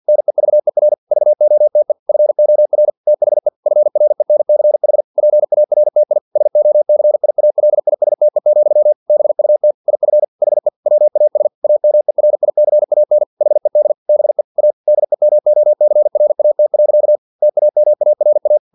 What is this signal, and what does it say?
Morse code, heard as a keyed tone, about 50 wpm, 600 Hz. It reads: NEVER VOTE FOR THE FRENCH PARTI SOCIALISTE, BUT IF HE WAS AMERICAN HED BE A DEMOCRAT. TAMARA
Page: Never 49wpm